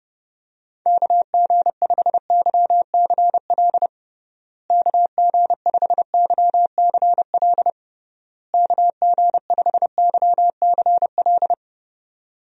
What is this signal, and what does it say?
Morse code, heard as a keyed tone, about 30 wpm, 700 Hz. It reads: KG5YCL KG5YCL KG5YCL
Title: USA Call Signs - 6 Characters - Repeated 3x - 30wpm